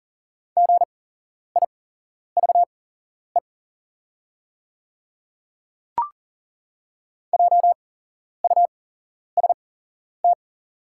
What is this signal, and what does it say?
Morse code, heard as a keyed tone, about 40 wpm, spaced as for 11 wpm, 690 Hz. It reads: GIVE JUST
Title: 8x Character Spacing - Top 200 Words - 40wpm